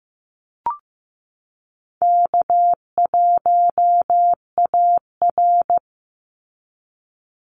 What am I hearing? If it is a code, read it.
Morse code, heard as a keyed tone, about 15 wpm, 700 Hz. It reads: K1AR
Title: Easy Call Signs Repeated 3x - 15wpm